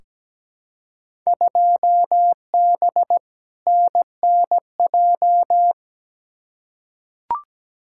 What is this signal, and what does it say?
Morse code, heard as a keyed tone, about 17 wpm, 700 Hz. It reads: 2B NNJ
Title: Field Day Exchanges Encore - 17wpm